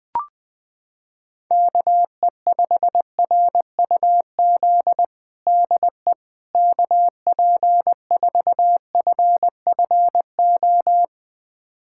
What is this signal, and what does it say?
Morse code, heard as a keyed tone, about 20 wpm, 700 Hz. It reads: KE5RUZ DE KP4FFO
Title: USA Callsign DE Callsign - 6 Characters - 20wpm